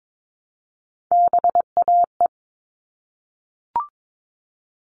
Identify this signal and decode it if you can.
Morse code, heard as a keyed tone, about 22 wpm, 700 Hz. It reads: BAE
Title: All Cut Numbers - 3-Digit 22wpm